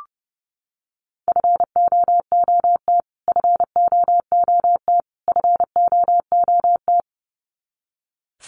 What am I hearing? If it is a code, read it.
Morse code, heard as a keyed tone, about 30 wpm, 700 Hz. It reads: FOOT FOOT FOOT
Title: Top 500 Words Repeated 30wpm